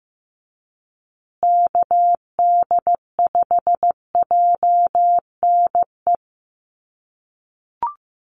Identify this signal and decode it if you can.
Morse code, heard as a keyed tone, about 15 wpm, 700 Hz. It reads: KD5JNE
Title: USA Call Signs Repeated 3x 15wpm